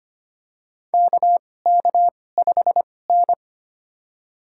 Morse code, heard as a keyed tone, about 25 wpm, spaced as for 18 wpm, 700 Hz. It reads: KK5N
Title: USA Call Signs - 2x Character Spacing - 25wpm